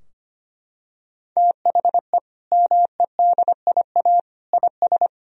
Morse code, heard as a keyed tone, about 25 wpm, 700 Hz. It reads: THE MEDIA IS
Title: Sentences from Top 1000 Words - Review Part 4 - 25wpm